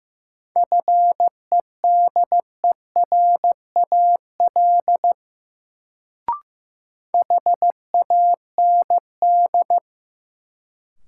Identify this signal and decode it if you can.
Morse code, heard as a keyed tone, about 15 wpm, 700 Hz. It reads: FEDERAL HAND
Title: Sets of 3 Words - New Word Drill - 15wpm